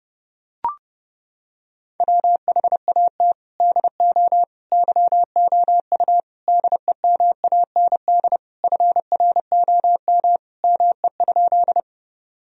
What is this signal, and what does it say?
Morse code, heard as a keyed tone, about 32 wpm, 700 Hz. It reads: WHAT DO YOU DEMAND FROM ME?
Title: Sentences from Top 1000 Words - Review Part 4 - 30wpm